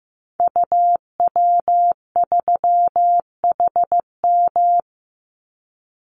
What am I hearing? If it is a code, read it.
Morse code, heard as a keyed tone, about 15 wpm, 700 Hz. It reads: UW3HM
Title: Call Signs Repeated 3x - 15wpm